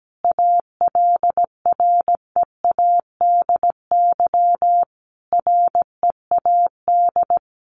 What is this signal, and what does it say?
Morse code, heard as a keyed tone, about 17 wpm, 700 Hz. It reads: ALREADY READ
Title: Sentences from Top 700 Words 17wpm